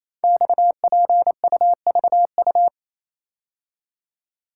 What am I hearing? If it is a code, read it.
Morse code, heard as a keyed tone, about 28 wpm, 700 Hz. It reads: XPUVU
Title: Code Groups - 5 Letters - 28wpm